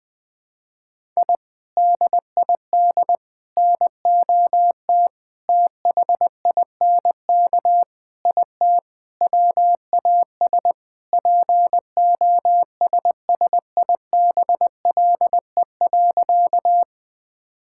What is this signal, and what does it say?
Morse code, heard as a keyed tone, about 20 wpm, 700 Hz. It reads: I DID NOT THINK IT WAS POSSIBLE.
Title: Sentences with 4-Word Phrases - 20wpm